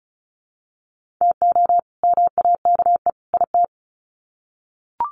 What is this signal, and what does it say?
Morse code, heard as a keyed tone, about 34 wpm, 700 Hz. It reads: TO MAKE IT
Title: Sets of 3 Words 35wpm